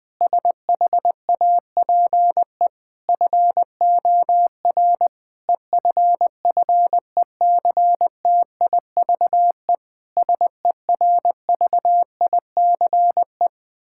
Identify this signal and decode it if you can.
Morse code, heard as a keyed tone, about 20 wpm, 700 Hz. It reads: SHAPE FOR EFFECTIVE SERVICE
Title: Sentences from Top 1000 Words - Review Part 6 - 20wpm